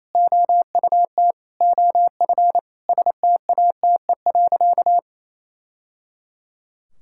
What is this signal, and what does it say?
Morse code, heard as a keyed tone, about 28 wpm, 700 Hz. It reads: OUT OF STATE.
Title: Sentences from Top 1000 Words - Review Part 5 - 28wpm